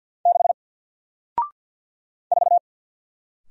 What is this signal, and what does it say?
Morse code, heard as a keyed tone, about 48 wpm, 700 Hz. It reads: 6 4
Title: Single Character (Letter:Number:Punctuation:Prosign) Alt 50wpm